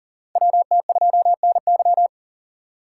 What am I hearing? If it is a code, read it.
Morse code, heard as a keyed tone, about 40 wpm, 700 Hz. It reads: WT2NY
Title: USA Call Signs - 4x Character Spacing - 40wpm